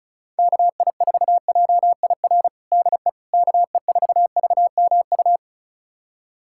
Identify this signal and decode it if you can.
Morse code, heard as a keyed tone, about 34 wpm, 700 Hz. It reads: KI4JIR DE KE4VMU
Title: USA Callsign DE Callsign - 6 Characters - 35wpm